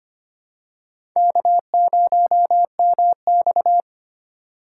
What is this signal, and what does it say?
Morse code, heard as a keyed tone, about 25 wpm, 700 Hz. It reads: K0MX
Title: USA Call Signs - 3x Character Spacing - 25wpm